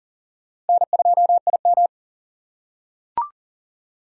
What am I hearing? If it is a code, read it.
Morse code, heard as a keyed tone, about 40 wpm, 700 Hz. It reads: N2IM
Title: USA Call Signs - 4 Characters - 4x Character Spacing - 40wpm